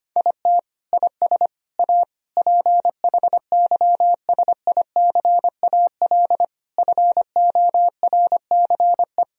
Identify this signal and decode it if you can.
Morse code, heard as a keyed tone, about 25 wpm, 700 Hz. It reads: IT IS A PHYSICAL FORCE
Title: Sentences from Top 1000 Words - Review Part 4 - 25wpm